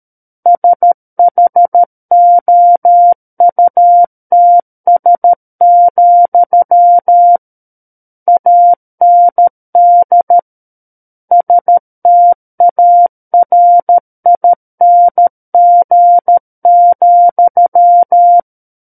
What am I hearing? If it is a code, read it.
Morse code, heard as a keyed tone, about 13 wpm, 700 Hz. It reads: SHOUTS, AND STARING,
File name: Book One / 04-Chapter 4 - 13 WPM